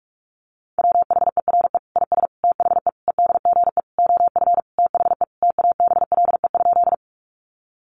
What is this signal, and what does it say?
Morse code, heard as a keyed tone, about 44 wpm, 700 Hz. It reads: WHERE IS THE EDGE OF THE TABLE?
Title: Sentences from Top 1300 Words 45wpm